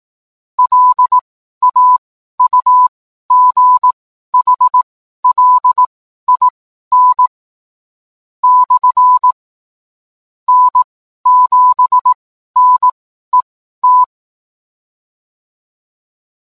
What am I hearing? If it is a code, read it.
Morse code, heard as a keyed tone, about 18 wpm, spaced as for 12 wpm, 1000 Hz. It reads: LAUGHLIN / N7NET